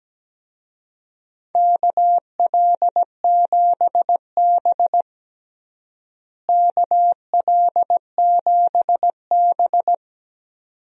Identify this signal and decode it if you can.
Morse code, heard as a keyed tone, about 17 wpm, 700 Hz. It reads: KL7B KL7B
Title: DXCC-100 Call Signs - 4 Characters - Repeated 2x - 17wpm